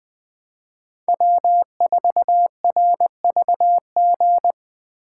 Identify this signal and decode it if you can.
Morse code, heard as a keyed tone, about 20 wpm, 700 Hz. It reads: W4RVG
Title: Speed Racing - USA 1x3 Call Signs 17wpm